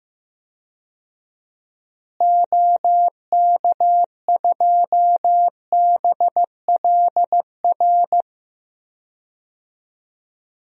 Morse code, heard as a keyed tone, about 15 wpm, 700 Hz. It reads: OK2BLR
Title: DXCC-100 Call Signs - 6 Characters - Repeated 3x - 15wpm